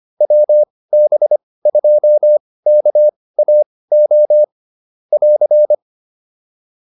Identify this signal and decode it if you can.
Morse code, heard as a keyed tone, about 25 wpm, spaced as for 18 wpm, 600 Hz. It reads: WB2KAO <AR>
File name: POTA208 18